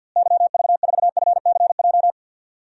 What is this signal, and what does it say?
Morse code, heard as a keyed tone, about 49 wpm, 700 Hz. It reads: YV4FCJ
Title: DXCC-100 Call Signs - 6 Characters - Repeated 3x - 50wpm